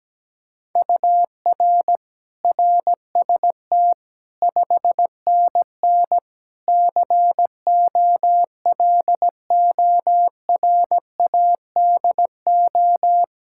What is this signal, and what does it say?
Morse code, heard as a keyed tone, about 17 wpm, 700 Hz. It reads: UR RST 5NN COLORADO
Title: Practice QSO - Exchange 1 - Beginning - 17wpm